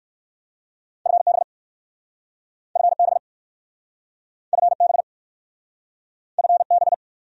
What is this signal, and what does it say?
Morse code, heard as a keyed tone, about 60 wpm, 700 Hz. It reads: FB FB FB FB